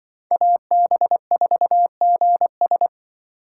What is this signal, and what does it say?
Morse code, heard as a keyed tone, about 24 wpm, 700 Hz. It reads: AB4GS
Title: Speed Racing - USA 2x2 Call Signs 20wpm